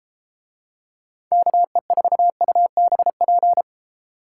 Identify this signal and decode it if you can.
Morse code, heard as a keyed tone, about 30 wpm, 700 Hz. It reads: KE4UBP